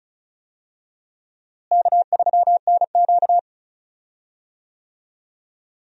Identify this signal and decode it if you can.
Morse code, heard as a keyed tone, about 36 wpm, 700 Hz. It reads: K3NQ
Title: USA Call Signs - 4 Characters - Repeated 2x - 35wpm